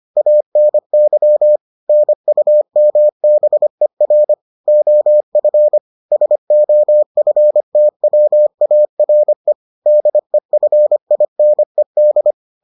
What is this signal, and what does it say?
Morse code, heard as a keyed tone, about 25 wpm, 600 Hz. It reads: ANY NUMBER OF SOFTWARE DEFINED